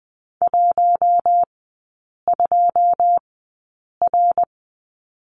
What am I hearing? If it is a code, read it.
Morse code, heard as a keyed tone, about 20 wpm, spaced as for 8 wpm, 700 Hz. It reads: 12R
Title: 087 Lesson 15e - Number 5 - Three Characters 20wpm